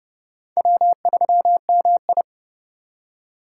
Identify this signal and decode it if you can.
Morse code, heard as a keyed tone, about 29 wpm, 700 Hz. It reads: W3MI
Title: USA Call Signs - 2x Character Spacing - 30wpm